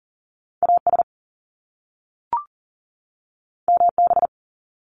Morse code, heard as a keyed tone, about 38 wpm, 700 Hz. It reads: AS MB
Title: Top Two Letter Combinations in Words 40wpm